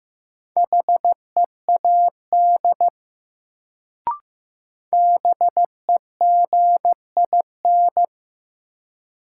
Morse code, heard as a keyed tone, about 15 wpm, 700 Hz. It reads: HEAD BEGIN
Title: Sentences from Top 300 Words - New Word Drill - 15wpm